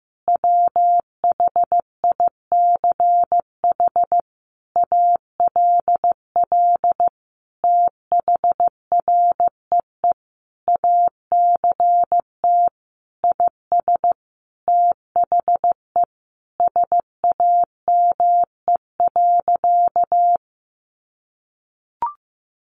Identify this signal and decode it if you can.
Morse code, heard as a keyed tone, about 15 wpm, 700 Hz. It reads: WHICH ALL THREE ACT IS THE SAME.
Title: Sentences from Top 1400 Words 15wpm